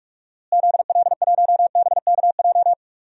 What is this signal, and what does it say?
Morse code, heard as a keyed tone, about 46 wpm, 700 Hz. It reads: ZL1BKJ
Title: DXCC-100 Call Signs - 6 Characters - Repeated 2x - 45wpm